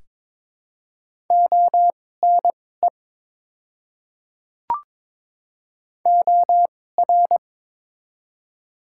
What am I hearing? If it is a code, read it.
Morse code, heard as a keyed tone, about 22 wpm, spaced as for 16 wpm, 700 Hz. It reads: ONE OR
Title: 2x Character Spacing - Top 200 Words - 22wpm